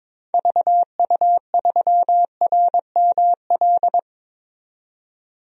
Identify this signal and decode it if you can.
Morse code, heard as a keyed tone, about 22 wpm, 700 Hz. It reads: VU3RML
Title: DXCC-100 Call Signs - 6 Characters - 22wpm